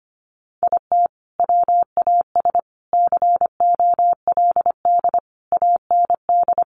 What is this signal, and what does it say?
Morse code, heard as a keyed tone, about 25 wpm, 700 Hz. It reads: IT WAS COLD AND
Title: Sentences from Top 500 Words 25wpm